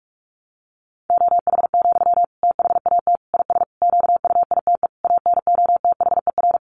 Morse code, heard as a keyed tone, about 44 wpm, 700 Hz. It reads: OH, THAT IS QUITE ANOTHER
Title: Sentences from Top 700 Words 45wpm